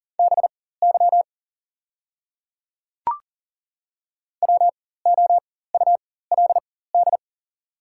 Morse code, heard as a keyed tone, about 40 wpm, spaced as for 19 wpm, 690 Hz. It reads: BY WOULD
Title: 4x Character Spacing - Top 100 Words - 40wpm